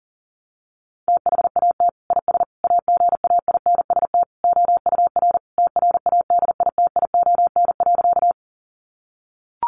Morse code, heard as a keyed tone, about 40 wpm, 700 Hz. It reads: THAT IS AGAINST OUR TRADITION.
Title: Sentences from Top 1500 Words 40wpm